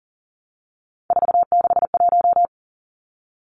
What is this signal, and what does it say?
Morse code, heard as a keyed tone, about 40 wpm, 700 Hz. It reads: 461
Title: Numbers - 3-Digit 40wpm